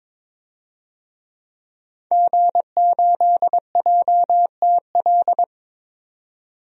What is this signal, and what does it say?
Morse code, heard as a keyed tone, about 21 wpm, 700 Hz. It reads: G8JTL
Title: DXCC-100 Call Signs - 5 Characters - 22wpm